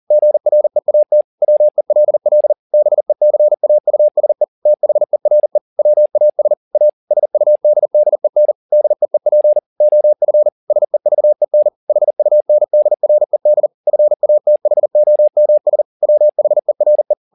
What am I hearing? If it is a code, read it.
Morse code, heard as a keyed tone, about 40 wpm, 600 Hz. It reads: GREAT WELL BECAUSE THERE WAS A SUDDEN DEEP OF SEVEN HUNDRED FATHOMS WHERE